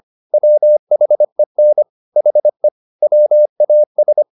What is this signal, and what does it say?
Morse code, heard as a keyed tone, about 25 wpm, 600 Hz. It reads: WHEN HE WAS
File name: the old man and the sea part 1 25wpm